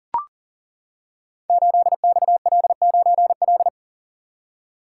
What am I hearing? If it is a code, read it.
Morse code, heard as a keyed tone, about 38 wpm, 700 Hz. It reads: 8XL9L
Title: Code Groups - 5 Letter:Numbers - 40wpm